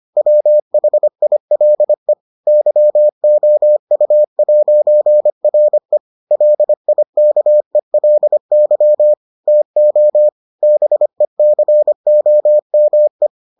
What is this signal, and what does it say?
Morse code, heard as a keyed tone, about 25 wpm, 600 Hz. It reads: WHILE YOU'RE LIKELY TO BECOME